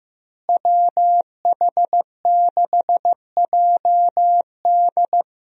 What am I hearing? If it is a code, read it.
Morse code, heard as a keyed tone, about 15 wpm, 700 Hz. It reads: WH6JD
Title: USA Call Signs - 5 Characters - Repeated 3x - 15wpm